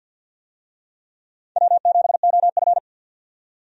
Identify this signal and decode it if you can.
Morse code, heard as a keyed tone, about 48 wpm, 700 Hz. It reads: W7OF
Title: USA Call Signs - 4 Characters - 50wpm